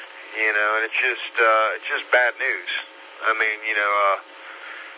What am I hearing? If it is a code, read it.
Emergency Alert System (SAME) tones, heard as harsh buzzing bursts, with speech. It is heard from 0.2 to 3.8 s.